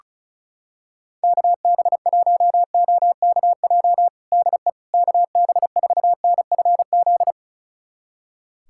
Morse code, heard as a keyed tone, about 36 wpm, 700 Hz. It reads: KB1OKJ DE KB4NFZ
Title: USA Callsign DE Callsign 35wpm